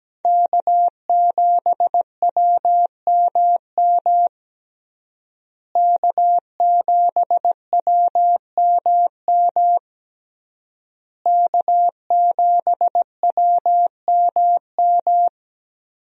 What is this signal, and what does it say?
Morse code, heard as a keyed tone, about 17 wpm, 700 Hz. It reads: K7WMM K7WMM K7WMM
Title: USA Call Signs - 5 Characters - Repeated 3x - 17wpm